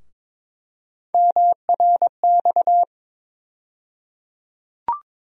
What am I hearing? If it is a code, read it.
Morse code, heard as a keyed tone, about 22 wpm, 700 Hz. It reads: MRX
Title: USA Call Sign Postfixes - 3 Letters 22wpm